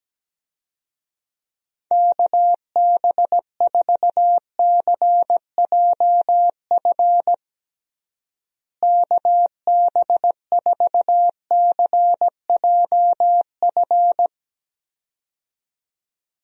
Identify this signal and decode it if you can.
Morse code, heard as a keyed tone, about 17 wpm, 700 Hz. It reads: KB4CJF KB4CJF
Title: USA Call Signs - 6 Characters - Repeated 2x - 17wpm